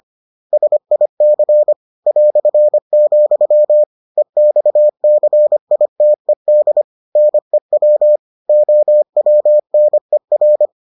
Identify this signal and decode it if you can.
Morse code, heard as a keyed tone, about 25 wpm, 600 Hz. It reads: SIC ", EXCITED NEW OWNER